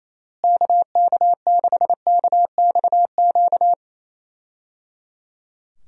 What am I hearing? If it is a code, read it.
Morse code, heard as a keyed tone, about 28 wpm, 700 Hz. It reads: KK6KXQ
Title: USA Call Signs - 6 Characters - Repeated 3x - 28wpm